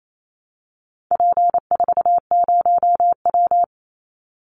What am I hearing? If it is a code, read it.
Morse code, heard as a keyed tone, about 28 wpm, 700 Hz. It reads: P40W
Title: Call Signs Repeated 2x - 28wpm